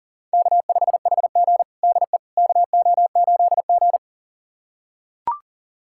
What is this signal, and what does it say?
Morse code, heard as a keyed tone, about 40 wpm, 700 Hz. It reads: K5HG DE KO8G
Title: USA Callsign DE Callsign - 4 Characters - 40wpm